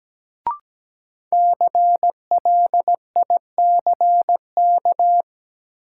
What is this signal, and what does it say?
Morse code, heard as a keyed tone, about 17 wpm, 700 Hz. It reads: CLICK
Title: Sentences from Top 1100 Words - New Word Drill - 17wpm